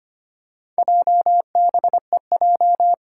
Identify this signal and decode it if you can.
Morse code, heard as a keyed tone, about 25 wpm, 700 Hz. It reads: JBEJ
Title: Four Challenging Letter-Number Combinations 25wpm